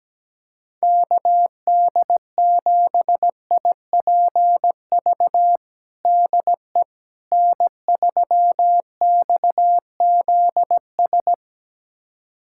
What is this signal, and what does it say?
Morse code, heard as a keyed tone, about 17 wpm, 700 Hz. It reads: KD7IPV DE N3XZS
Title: USA Callsign DE Callsign 17wpm